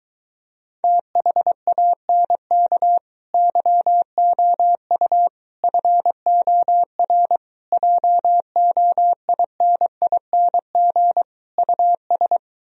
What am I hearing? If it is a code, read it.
Morse code, heard as a keyed tone, about 22 wpm, 700 Hz. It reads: THANK YOU FOR JOINING US